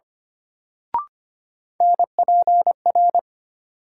Morse code, heard as a keyed tone, about 25 wpm, 700 Hz. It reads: NPR
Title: Sets of 5 Words - New Word Drill - 25wpm